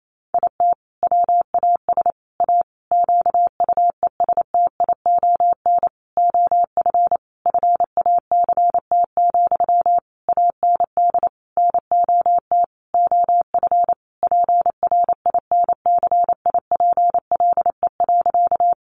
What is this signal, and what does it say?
Morse code, heard as a keyed tone, about 28 wpm, 700 Hz. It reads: IT WAS A QUESTION OF FACT, AND NOT OF PRINCIPLE.
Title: Sentences from Top 1400 Words 28wpm